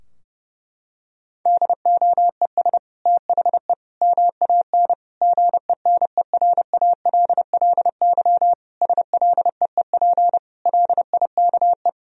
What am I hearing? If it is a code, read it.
Morse code, heard as a keyed tone, about 30 wpm, 700 Hz. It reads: DOES THE MAN GENERALLY SLEEP LIKE
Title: Sentences from Top 1000 Words - Review Part 4 - 30wpm